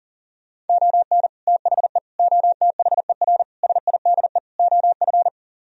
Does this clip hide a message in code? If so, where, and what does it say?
Yes: Morse code from 0.7 to 5.3 s, heard as a keyed tone, about 38 wpm, 700 Hz. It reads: ON THE OTHER SIDE OF